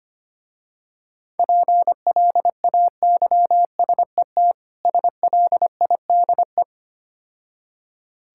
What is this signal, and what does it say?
Morse code, heard as a keyed tone, about 25 wpm, 700 Hz. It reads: PLAYSET SLIDE
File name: practice / 25wpm / Difficult Words - Speed-Racing with Hints - 25wpm